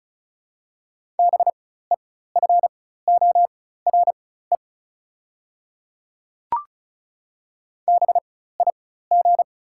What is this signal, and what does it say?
Morse code, heard as a keyed tone, about 36 wpm, spaced as for 16 wpm, 700 Hz. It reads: BEFORE BIG
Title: 4x Character Spacing - Top 300 Words - 35wpm